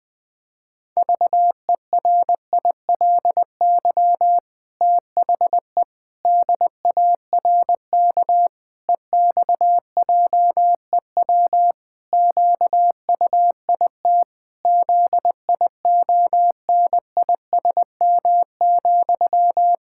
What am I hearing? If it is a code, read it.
Morse code, heard as a keyed tone, about 20 wpm, 700 Hz. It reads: VERILY THE DARK EXJEW QUIT ZIONISM,
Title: Pangram Challenge v2 - 20wpm